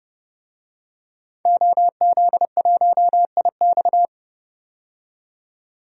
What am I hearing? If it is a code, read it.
Morse code, heard as a keyed tone, about 31 wpm, 700 Hz. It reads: OZ1IX
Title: DXCC-100 Call Signs - 5 Characters - 30wpm